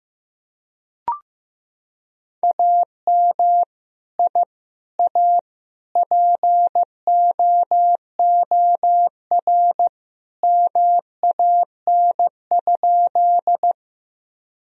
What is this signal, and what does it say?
Morse code, heard as a keyed tone, about 15 wpm, 700 Hz. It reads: AM I A POOR MAN?
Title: Sentences from Top 1000 Words - Review Part 2 - 15wpm